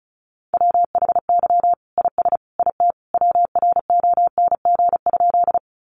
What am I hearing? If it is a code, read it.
Morse code, heard as a keyed tone, about 34 wpm, 700 Hz. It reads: WHY IS IT WRONG?
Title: Sentences from Top 900 Words 35wpm